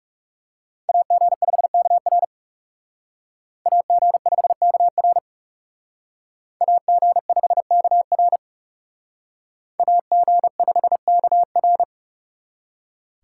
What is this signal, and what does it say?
Morse code, heard as a keyed tone, about 38 wpm, 700 Hz. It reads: AG5KR AG5KR AG5KR AG5KR